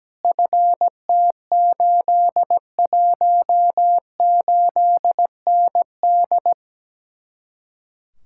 Morse code, heard as a keyed tone, about 17 wpm, 700 Hz. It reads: FT818ND
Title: Speed-Racing Common QSO Phrases - 17wpm